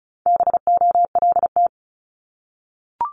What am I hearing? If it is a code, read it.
Morse code, heard as a keyed tone, about 36 wpm, 700 Hz. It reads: BOLT
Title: Multi-letter Phonemes Repeated 3x 35wpm